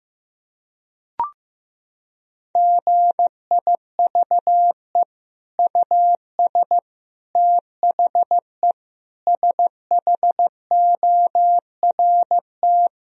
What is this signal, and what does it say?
Morse code, heard as a keyed tone, about 15 wpm, 700 Hz. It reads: GIVE US THE SHORT
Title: Sentences from Top 1000 Words - Review Part 4 - 15wpm